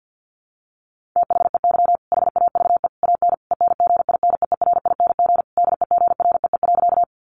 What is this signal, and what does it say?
Morse code, heard as a keyed tone, about 50 wpm, 700 Hz. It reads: THEY HAVE AN ENGINEERING DEGREE.
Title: Sentences from Top 1500 Words 50wpm